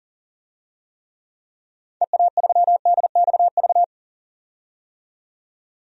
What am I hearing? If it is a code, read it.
Morse code, heard as a keyed tone, about 40 wpm, 700 Hz. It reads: EA3DXV
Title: DXCC-100 Call Signs - 6 Characters - 40wpm